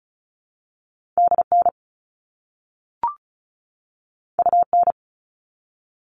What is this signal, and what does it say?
Morse code, heard as a keyed tone, about 36 wpm, 700 Hz. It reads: DN UN
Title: All Cut Numbers - 2-Digit 35wpm